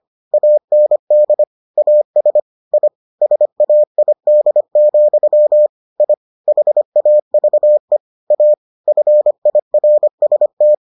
Morse code, heard as a keyed tone, about 25 wpm, 600 Hz. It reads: AND AS I SAID, I HAVE A FIRST